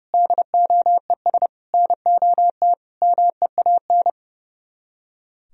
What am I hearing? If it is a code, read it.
Morse code, heard as a keyed tone, about 37 wpm, 700 Hz. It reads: DOES NOT MEAN
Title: Speed-Racing - Sets of 5 Words 30wpm